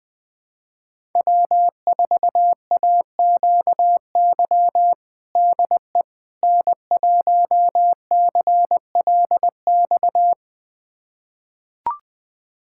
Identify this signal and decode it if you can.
Morse code, heard as a keyed tone, about 20 wpm, 700 Hz. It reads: W4AQY DE N1CLX
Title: USA Callsign DE Callsign - 5 Characters - 20wpm